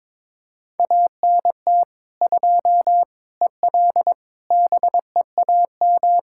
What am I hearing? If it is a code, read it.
Morse code, heard as a keyed tone, about 22 wpm, 700 Hz. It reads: ANT 2 EL BEAM
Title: K1USN SST QSO Practice 22wpm